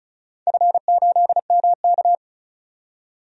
Morse code, heard as a keyed tone, about 36 wpm, 700 Hz. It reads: F8MK
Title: DXCC-100 Call Signs - 4 Characters - 35wpm